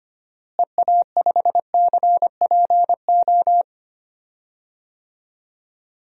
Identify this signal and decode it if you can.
Morse code, heard as a keyed tone, about 25 wpm, 700 Hz. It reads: EA5CPO
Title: DXCC-100 Call Signs - 6 Characters - Repeated 2x - 25wpm